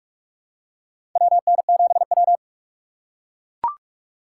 Morse code, heard as a keyed tone, about 41 wpm, 700 Hz. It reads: WN7W
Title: USA Call Signs - 4 Characters - 2x Character Spacing - 45wpm